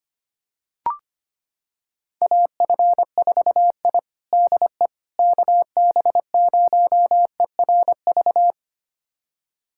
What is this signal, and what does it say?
Morse code, heard as a keyed tone, about 25 wpm, 700 Hz. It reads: AF4I DE KB0ERV
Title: USA Callsign DE Callsign 25wpm